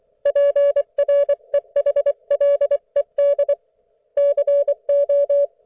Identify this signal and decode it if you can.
Morse code, heard as a keyed tone, about 24 wpm, 570 Hz. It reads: PREHLED CO
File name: Archiv / CUC224 / ok1fhi cul224